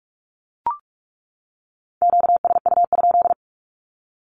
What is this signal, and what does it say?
Morse code, heard as a keyed tone, about 44 wpm, 700 Hz. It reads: QSU?
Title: Q-Codes 45wpm